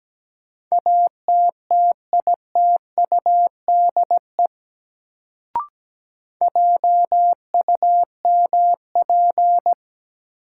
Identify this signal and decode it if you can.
Morse code, heard as a keyed tone, about 17 wpm, 700 Hz. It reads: ATTITUDE JUMP
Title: Sentences from Top 1300 Words - New Word Drill - 17wpm